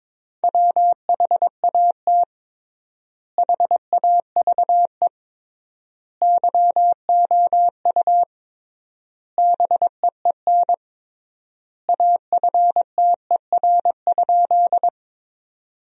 Morse code, heard as a keyed tone, about 22 wpm, 700 Hz. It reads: WHAT HAVE YOU BEEN AFTER?
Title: 3x Word-Spacing - Sentences from Top 200 Words - 22wpm